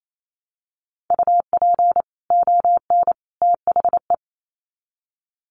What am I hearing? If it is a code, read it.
Morse code, heard as a keyed tone, about 28 wpm, 700 Hz. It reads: UP ON THE
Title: Sets of 3 Words 28wpm